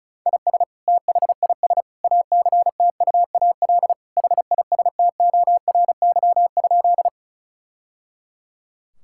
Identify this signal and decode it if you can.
Morse code, heard as a keyed tone, about 36 wpm, 700 Hz. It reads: IS THIS ACTUAL HISTORY?
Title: Sentences from Top 1000 Words - Review Part 5 - 35wpm